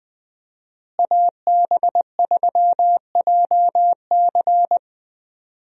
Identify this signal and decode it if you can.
Morse code, heard as a keyed tone, about 20 wpm, 700 Hz. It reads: AB3JC
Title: USA Call Signs - 5 Characters - Repeated 2x - 20wpm